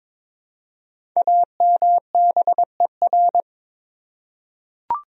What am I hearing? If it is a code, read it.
Morse code, heard as a keyed tone, about 22 wpm, 700 Hz. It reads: AMBER
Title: USA Common Names - Max 5 Letters - 22wpm